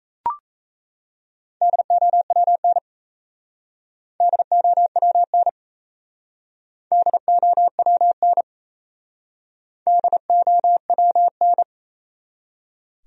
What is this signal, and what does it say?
Morse code, heard as a keyed tone, about 38 wpm, 700 Hz. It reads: DOWN DOWN DOWN DOWN